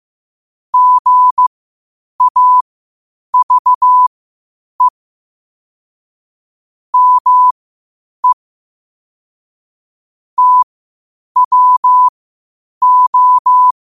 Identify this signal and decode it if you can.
Morse code, heard as a keyed tone, about 15 wpm, spaced as for 8 wpm, 1000 Hz. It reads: GAVE ME TWO
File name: 8wpm